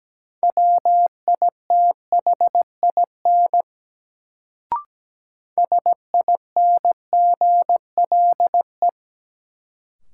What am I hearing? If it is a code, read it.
Morse code, heard as a keyed tone, about 17 wpm, 700 Hz. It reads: WITHIN SINGLE
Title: Sentences from Top 700 Words - New Word Drill - 17wpm